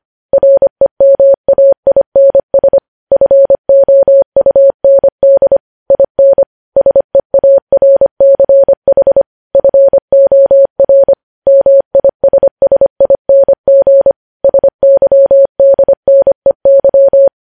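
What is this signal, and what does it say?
Morse code, heard as a keyed tone, about 25 wpm, 550 Hz. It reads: REMAINS FOUND IN SEARCH FOR MISSING SYDNEY
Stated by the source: Morse code news headlines at 25 wpm.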